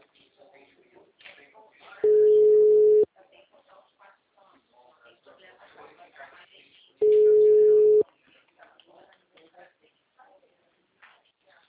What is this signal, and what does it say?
Ringback tone (Europe, 425 Hz), with speech.